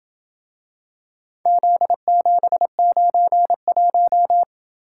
Morse code, heard as a keyed tone, about 27 wpm, 700 Hz. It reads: Z791